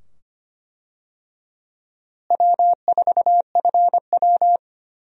Morse code, heard as a keyed tone, about 25 wpm, 700 Hz. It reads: W4FW
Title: USA Call Signs - 4 Characters - Repeated 2x - 25wpm